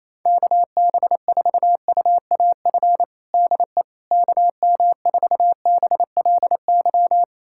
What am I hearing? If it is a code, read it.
Morse code, heard as a keyed tone, about 28 wpm, 700 Hz. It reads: KB4UAF DE KM4BLY
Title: USA Callsign DE Callsign 28wpm